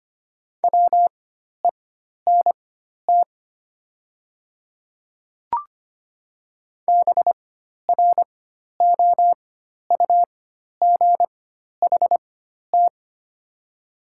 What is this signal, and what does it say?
Morse code, heard as a keyed tone, about 25 wpm, spaced as for 12 wpm, 690 Hz. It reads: WENT BROUGHT
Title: 4x Character Spacing - Top 500 Words - 25wpm